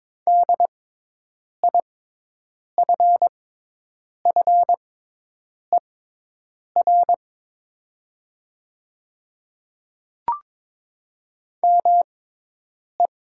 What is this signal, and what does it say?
Morse code, heard as a keyed tone, about 22 wpm, spaced as for 8 wpm, 700 Hz. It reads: DIFFER ME
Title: 6x Character Spacing - Top 300 Words - 22wpm